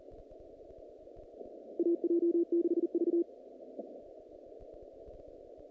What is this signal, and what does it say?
Morse code, heard as a keyed tone, about 40 wpm, 340 Hz. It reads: AJ6V
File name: contests / naqp-cw-k5tr-op-k5pi- -2200